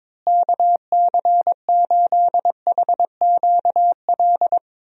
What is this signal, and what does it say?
Morse code, heard as a keyed tone, about 22 wpm, 700 Hz. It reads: KC8HQL
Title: USA Call Signs - 6 Characters - Repeated 2x - 22wpm